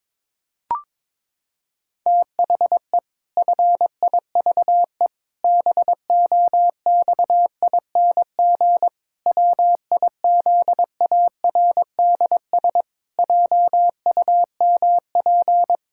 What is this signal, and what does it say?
Morse code, heard as a keyed tone, about 22 wpm, 700 Hz. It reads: THE FIVE BOXING WIZARDS JUMP
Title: Pangram Challenge v2 - 22wpm